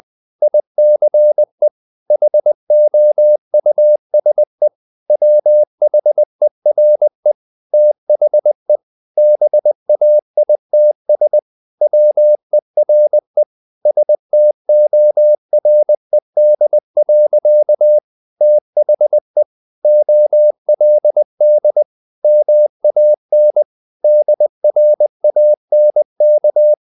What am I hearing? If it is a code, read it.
Morse code, heard as a keyed tone, about 20 wpm, 600 Hz. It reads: ICE HOUSE WHERE THE BAITS WERE STORED. THE OLD MAN DRANK
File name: the old man and the sea / the old man and the sea part 1 20wpm